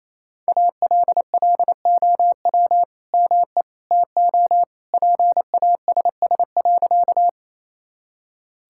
Morse code, heard as a keyed tone, about 28 wpm, 700 Hz. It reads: ALLOW ME TO PASS.
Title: Sentences from Top 1000 Words - Review Part 5 - 28wpm